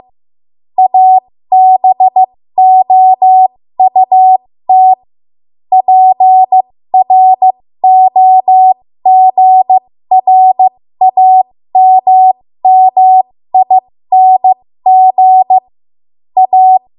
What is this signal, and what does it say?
Morse code, heard as a keyed tone, about 15 wpm, spaced as for 13 wpm, 750 Hz. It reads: ABOUT PROGRAMMING A
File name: Morse / 13WPM